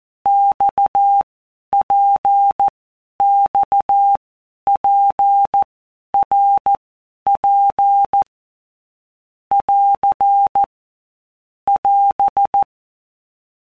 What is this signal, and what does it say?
Morse code, heard as a keyed tone, about 14 wpm, 800 Hz. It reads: X P X P R P <AR> <AS>
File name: All Groups 1-minute Blocks / Groups@13 / RKPX